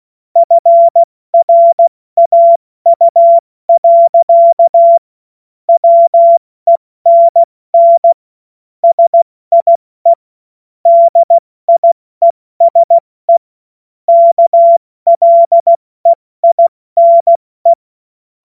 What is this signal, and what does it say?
Morse code, heard as a keyed tone, about 16 wpm, spaced as for 14 wpm, 680 Hz. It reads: FRAU. WENN SIE DIESE KLEINE
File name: Morsekurs / Witze / Witz 05 Dezember 2023 70 F80